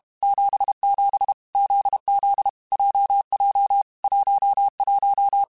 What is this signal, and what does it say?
Morse code, heard as a keyed tone, about 32 wpm, 800 Hz. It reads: 77 ZZ JJ 11
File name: Basic Letter Groups By7Groups / JZ17 / JZ17@30